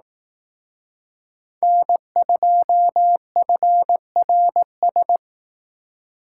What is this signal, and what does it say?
Morse code, heard as a keyed tone, about 18 wpm, 700 Hz. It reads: N2FRS
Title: Speed Racing - USA 1x3 Call Signs 15wpm